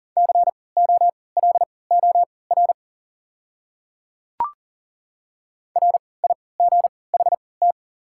Morse code, heard as a keyed tone, about 40 wpm, spaced as for 23 wpm, 700 Hz. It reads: COLOR RIGHT
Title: 3x Character Spacing - Top 300 Words - 40wpm